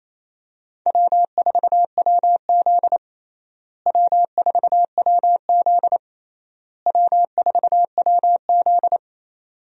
Morse code, heard as a keyed tone, about 28 wpm, 700 Hz. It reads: W4WZ W4WZ W4WZ
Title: USA Call Signs - 4 Characters - Repeated 3x - 28wpm